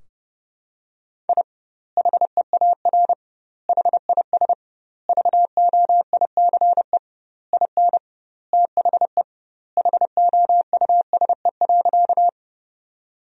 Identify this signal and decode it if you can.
Morse code, heard as a keyed tone, about 32 wpm, 700 Hz. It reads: I HEAR HIS VOICE IN THE HOUSE.
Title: 2x Word-Spacing - Sentences from Top 500 Words - 30wpm